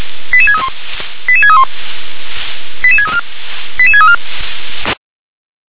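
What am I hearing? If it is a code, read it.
Five-tone selective call (ZVEI), heard as quick runs of five tones, tones of about 70 ms. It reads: 88631 88631 88634 88634
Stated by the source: ZVEI five-tone selective call received by SDR.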